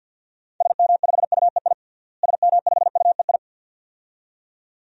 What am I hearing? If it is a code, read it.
Morse code, heard as a keyed tone, about 52 wpm, 700 Hz. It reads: SM5UEI SM5UEI
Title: DXCC-100 Call Signs - 6 Characters - Repeated 2x - 50wpm